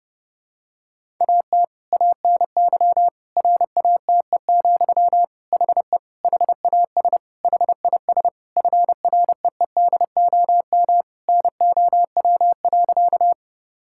Morse code, heard as a keyed tone, about 29 wpm, 700 Hz. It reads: AT ANY RATE, HE HAS HIS FREEDOM NOW.
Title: Sentences from Top 1000 Words - Review Part 5 - 30wpm